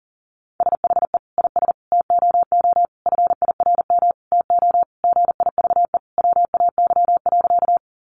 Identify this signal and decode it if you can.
Morse code, heard as a keyed tone, about 40 wpm, 700 Hz. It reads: SHE IS TOO FIRM TO GIVE WAY.
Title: Sentences from Top 1400 Words 40wpm